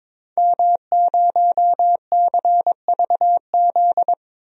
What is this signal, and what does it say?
Morse code, heard as a keyed tone, about 21 wpm, 700 Hz. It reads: M0CVZ
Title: DXCC-100 Call Signs - 5 Characters - 22wpm